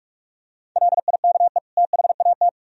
Morse code, heard as a keyed tone, about 45 wpm, 700 Hz. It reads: LIKE THAT